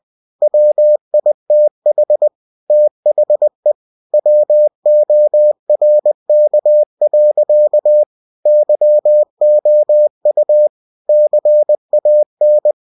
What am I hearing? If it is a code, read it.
Morse code, heard as a keyed tone, about 20 wpm, 600 Hz. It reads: WITH THE WORK. YOU CAN
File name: cw-books / winnie-the-pooh / winnie-20wpm-page-0114